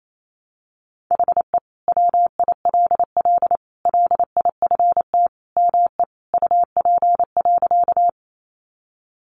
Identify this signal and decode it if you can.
Morse code, heard as a keyed tone, about 28 wpm, 700 Hz. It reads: HE WILL LIFT ME UP.
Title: Sentences from Top 1000 Words 28wpm